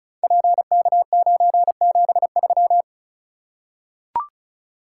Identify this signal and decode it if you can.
Morse code, heard as a keyed tone, about 36 wpm, 690 Hz. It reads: PK973
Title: Code Groups - 5 Letter:Numbers - 35wpm